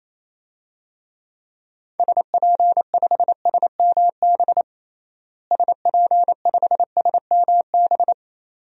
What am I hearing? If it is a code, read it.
Morse code, heard as a keyed tone, about 28 wpm, 700 Hz. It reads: SP5SMB SP5SMB
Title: DXCC-100 Call Signs - 6 Characters - Repeated 2x - 28wpm